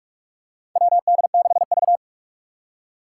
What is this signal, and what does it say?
Morse code, heard as a keyed tone, about 46 wpm, 700 Hz. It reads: WD6V
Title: Four Challenging Letter-Number Combinations 45wpm